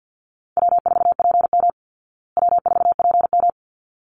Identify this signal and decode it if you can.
Morse code, heard as a keyed tone, about 50 wpm, 700 Hz. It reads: W4PM W4PM
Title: Easy Call Signs Repeated 2x - 50wpm